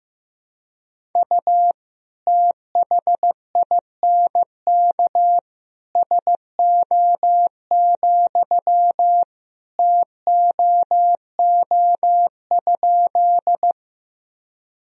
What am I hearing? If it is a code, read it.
Morse code, heard as a keyed tone, about 15 wpm, 700 Hz. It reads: U THINK SO, TOO?
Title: Sentences from Top 200 Words and Abbreviations 15wpm